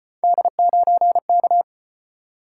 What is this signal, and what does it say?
Morse code, heard as a keyed tone, about 36 wpm, 700 Hz. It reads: D9K